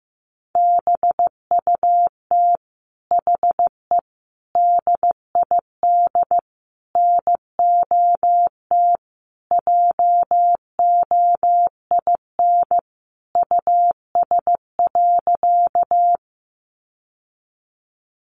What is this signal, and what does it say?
Morse code, heard as a keyed tone, about 14 wpm, 700 Hz. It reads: BUT HE DID NOT JOIN US.
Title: Sentences from Top 1400 Words 15wpm